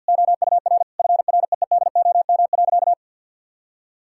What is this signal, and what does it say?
Morse code, heard as a keyed tone, about 51 wpm, 700 Hz. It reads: OUR FREEDOM.
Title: Sentences from Top 1000 Words - Review Part 5 - 50wpm